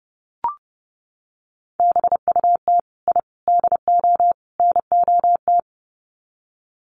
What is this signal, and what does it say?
Morse code, heard as a keyed tone, about 29 wpm, 700 Hz. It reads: BUT I DO NOT
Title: Sets of 4 Words 30wpm